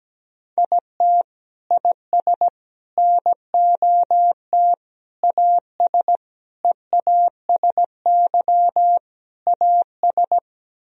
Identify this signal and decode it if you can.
Morse code, heard as a keyed tone, about 17 wpm, 700 Hz. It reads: IT IS NOT AS EASY AS
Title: Sentences from Top 1000 Words - Review Part 1 - 17wpm